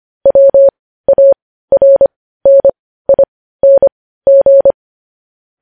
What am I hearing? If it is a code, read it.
Morse code, heard as a keyed tone, about 25 wpm, spaced as for 15 wpm, 550 Hz. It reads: WARNING
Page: news-headlines-in-morse-code-at-15-wpm